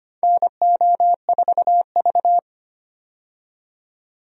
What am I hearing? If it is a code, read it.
Morse code, heard as a keyed tone, about 25 wpm, 700 Hz. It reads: NO4V
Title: USA Call Signs - 4 Characters - 25wpm